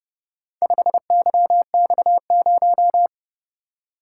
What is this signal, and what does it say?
Morse code, heard as a keyed tone, about 31 wpm, 700 Hz. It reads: 5YX0